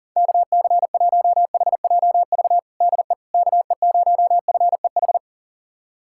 Morse code, heard as a keyed tone, about 37 wpm, 700 Hz. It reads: KC1HJV DE KE0FEH
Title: USA Callsign DE Callsign 40wpm